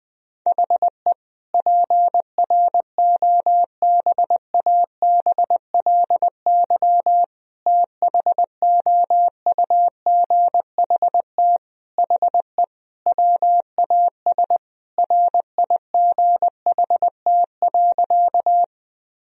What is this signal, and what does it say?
Morse code, heard as a keyed tone, about 21 wpm, 700 Hz. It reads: HE PROBABLY THOUGHT HE WAS RIGHT.
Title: Sentences from Top 1000 Words - Review Part 4 - 20wpm